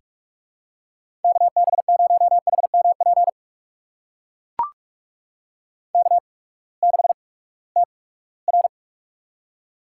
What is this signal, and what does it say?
Morse code, heard as a keyed tone, about 44 wpm, spaced as for 22 wpm, 700 Hz. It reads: KB0HMP K6TR
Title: USA Call Signs - 8x Character Spacing - 45wpm